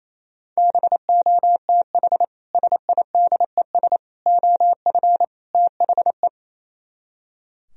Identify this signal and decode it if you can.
Morse code, heard as a keyed tone, about 34 wpm, 700 Hz. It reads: BOTH SIDES OF THE
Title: Speed-Racing - Sets of 4 Words 28wpm